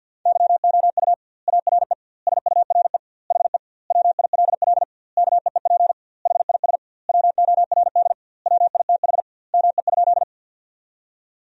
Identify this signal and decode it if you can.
Morse code, heard as a keyed tone, about 47 wpm, 700 Hz. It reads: YOU ARE SURE HE WILL KEEP HIS WORD WITH ME?
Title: 2x Word-Spacing - Sentences from Top 300 Words - 50wpm